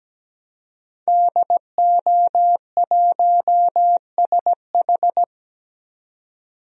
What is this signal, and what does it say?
Morse code, heard as a keyed tone, about 17 wpm, 700 Hz. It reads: DO1SH
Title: DXCC-100 Call Signs - 5 Characters - 17wpm